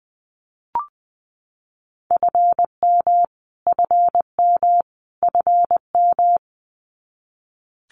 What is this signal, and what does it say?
Morse code, heard as a keyed tone, about 20 wpm, 700 Hz. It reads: FM FM FM
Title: Abbreviations Repeated 3x - 20wpm